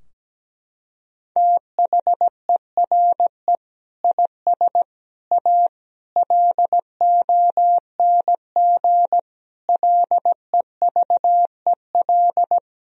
Morse code, heard as a keyed tone, about 17 wpm, 700 Hz. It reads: THERE IS A LONG LEVEL
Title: Sentences from Top 1000 Words - Review Part 4 - 17wpm